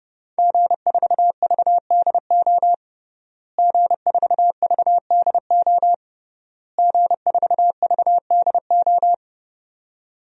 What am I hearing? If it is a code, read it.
Morse code, heard as a keyed tone, about 30 wpm, 700 Hz. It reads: G4VDO G4VDO G4VDO
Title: DXCC-100 Call Signs - 5 Characters - Repeated 3x - 30wpm